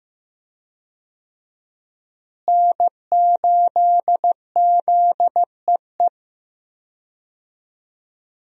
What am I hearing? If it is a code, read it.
Morse code, heard as a keyed tone, about 15 wpm, 700 Hz. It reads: N8ZEE
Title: USA Call Signs - 5 Characters - Repeated 3x - 15wpm